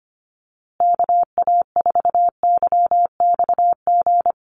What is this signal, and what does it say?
Morse code, heard as a keyed tone, about 25 wpm, 700 Hz. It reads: KA4YXG
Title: USA Call Signs Repeated 2x 25wpm